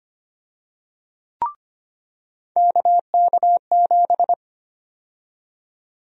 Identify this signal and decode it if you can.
Morse code, heard as a keyed tone, about 25 wpm, 700 Hz. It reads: KK7
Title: USA Call Sign Prefixes - 25wpm